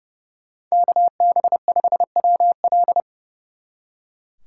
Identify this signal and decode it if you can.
Morse code, heard as a keyed tone, about 36 wpm, 700 Hz. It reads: KB5WL
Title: Speed Racing - USA 2x2 Call Signs 30wpm